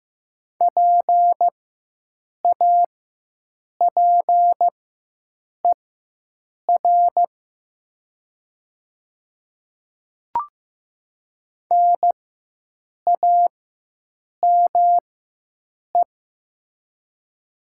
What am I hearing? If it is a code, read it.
Morse code, heard as a keyed tone, about 15 wpm, spaced as for 7 wpm, 700 Hz. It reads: PAPER NAME
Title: 4x Character Spacing - Top 300 Words - 15wpm